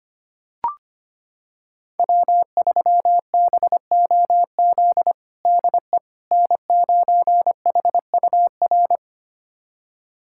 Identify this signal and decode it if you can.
Morse code, heard as a keyed tone, about 25 wpm, 700 Hz. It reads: W3BOZ DE N9HUR
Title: USA Callsign DE Callsign - 5 Characters - 25wpm